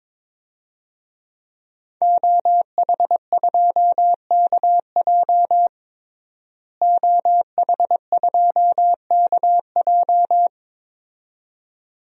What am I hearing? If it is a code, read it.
Morse code, heard as a keyed tone, about 22 wpm, 700 Hz. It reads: OH2KJ OH2KJ
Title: DXCC-100 Call Signs - 5 Characters - Repeated 2x - 22wpm